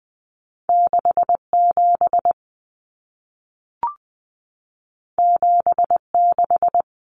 Morse code, heard as a keyed tone, about 20 wpm, 700 Hz. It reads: 67 76
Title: Common Cut Numbers - 2-Digit 20wpm